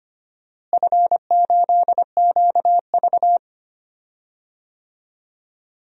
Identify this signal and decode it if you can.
Morse code, heard as a keyed tone, about 25 wpm, 700 Hz. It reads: F8QV
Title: DXCC-100 Call Signs - 4 Characters - Repeated 3x - 25wpm